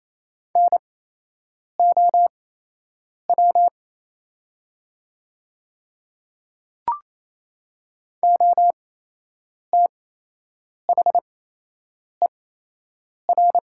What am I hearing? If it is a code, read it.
Morse code, heard as a keyed tone, about 28 wpm, spaced as for 8 wpm, 700 Hz. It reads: NOW OTHER
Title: 8x Character Spacing - Top 100 Words - 28wpm